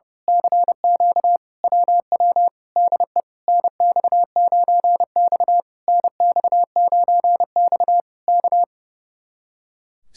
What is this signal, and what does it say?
Morse code, heard as a keyed tone, about 29 wpm, 700 Hz. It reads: CQ WW DE NX9X NX9X K
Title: Speed-Racing Common QSO Phrases Encore - 30wpm